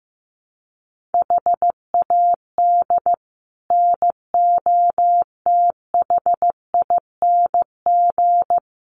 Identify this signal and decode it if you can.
Morse code, heard as a keyed tone, about 15 wpm, 700 Hz. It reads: HAD NOTHING
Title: Sets of 4 Words 15wpm